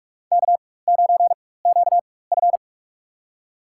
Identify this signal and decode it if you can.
Morse code, heard as a keyed tone, about 47 wpm, spaced as for 21 wpm, 700 Hz. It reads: K9QF
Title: USA Call Signs - 4 Characters - 4x Character Spacing - 45wpm